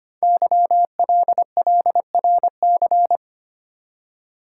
Morse code, heard as a keyed tone, about 25 wpm, 700 Hz. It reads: YLLRC
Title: Code Groups - 5 Letters - 25wpm